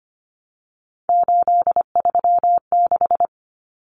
Morse code, heard as a keyed tone, about 25 wpm, 700 Hz. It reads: 836
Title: Numbers - 3-Digit 25wpm